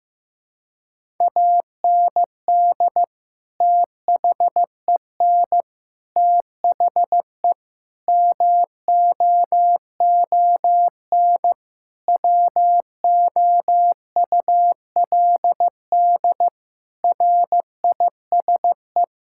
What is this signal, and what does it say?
Morse code, heard as a keyed tone, about 15 wpm, 700 Hz. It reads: AND THEN THE MOON WOULD RISE
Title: Sentences from Top 1000 Words - Review Part 6 - 15wpm